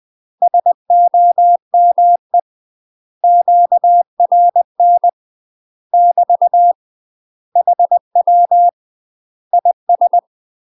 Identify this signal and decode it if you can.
Morse code, heard as a keyed tone, about 20 wpm, 700 Hz. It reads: SOME QRN <BT> HW IS